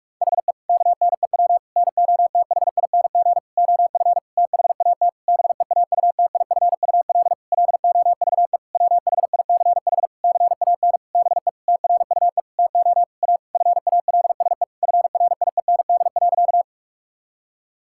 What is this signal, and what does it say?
Morse code, heard as a keyed tone, about 46 wpm, 700 Hz. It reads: HE KNEW NOTHING OF THAT BEAUTIFUL LOVE WHICH CAN BE TRUE TO A FALSE FRIEND.
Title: Sentences from Top 1600 Words 45wpm